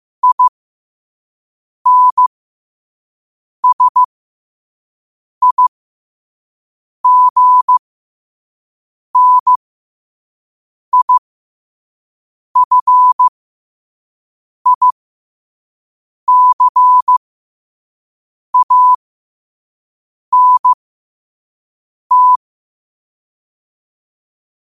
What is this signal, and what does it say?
Morse code, heard as a keyed tone, about 15 wpm, spaced as for 5 wpm, 1000 Hz. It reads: INSIGNIFICANT
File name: qotdfiles / 5wpm